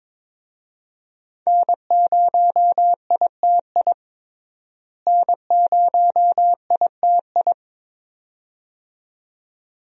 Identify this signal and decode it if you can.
Morse code, heard as a keyed tone, about 22 wpm, 700 Hz. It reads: N0ITI N0ITI
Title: USA Call Signs - 5 Characters - Repeated 2x - 22wpm